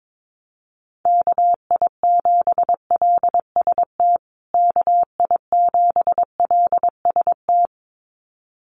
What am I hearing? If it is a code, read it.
Morse code, heard as a keyed tone, about 22 wpm, 700 Hz. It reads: KI7LST KI7LST
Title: USA Call Signs Repeated 2x 22wpm